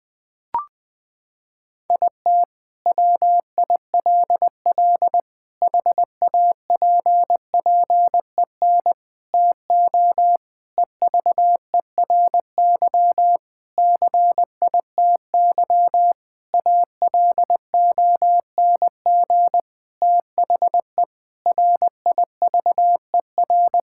Morse code, heard as a keyed tone, about 20 wpm, 700 Hz. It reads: IT WILL HAPPEN TO EVERY CITY ALONG THE RIVER
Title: Sentences from Top 1000 Words - Review Part 6 - 20wpm